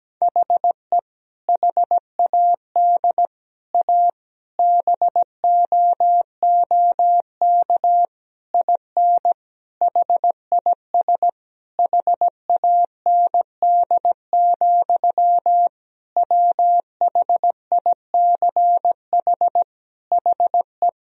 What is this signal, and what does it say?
Morse code, heard as a keyed tone, about 17 wpm, 700 Hz. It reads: HE HAD A BOOK IN HIS HAND, WHICH HE
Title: Sentences from Top 1000 Words - Review Part 3 - 17wpm